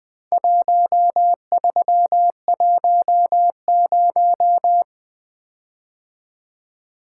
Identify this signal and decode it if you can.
Morse code, heard as a keyed tone, about 23 wpm, 700 Hz. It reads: 1310
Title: Speed-Racing - Four Letter-Number Combinations - 20wpm